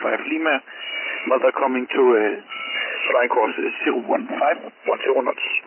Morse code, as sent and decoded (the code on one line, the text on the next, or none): none